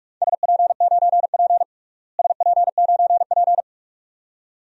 Morse code, heard as a keyed tone, about 46 wpm, 700 Hz. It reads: SP9P SP9P
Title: DXCC-100 Call Signs - 4 Characters - Repeated 2x - 45wpm